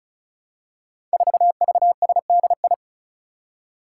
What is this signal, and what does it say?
Morse code, heard as a keyed tone, about 36 wpm, 700 Hz. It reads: 4VSDI
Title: Five Challenging Letter-Number Combinations 35wpm